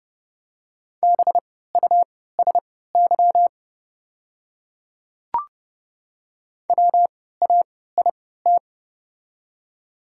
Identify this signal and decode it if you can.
Morse code, heard as a keyed tone, about 30 wpm, spaced as for 17 wpm, 700 Hz. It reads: BUSY WAIT
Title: 3x Character Spacing - Top 500 Words - 30wpm